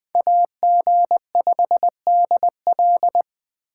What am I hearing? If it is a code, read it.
Morse code, heard as a keyed tone, about 20 wpm, 700 Hz. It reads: AG5DL
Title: Speed Racing - USA Call Signs 17wpm